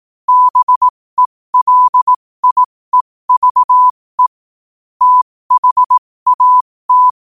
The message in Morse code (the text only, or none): BELIEVE THAT